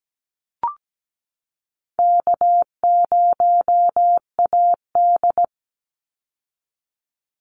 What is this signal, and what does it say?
Morse code, heard as a keyed tone, about 17 wpm, 700 Hz. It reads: K0AD
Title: Easy Call Signs Repeated 2x - 17wpm